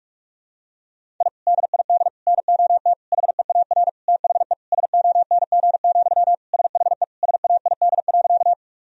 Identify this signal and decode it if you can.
Morse code, heard as a keyed tone, about 41 wpm, 700 Hz. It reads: I DID NOT HEAR THE SONG, SHE SAID.
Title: Sentences from Top 1000 Words - Review Part 2 - 45wpm